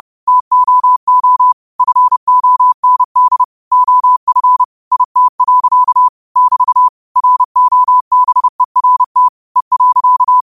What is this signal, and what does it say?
Morse code, heard as a keyed tone, about 30 wpm, 1000 Hz. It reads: TOO FOND OF IT. <BT> ROBERT E.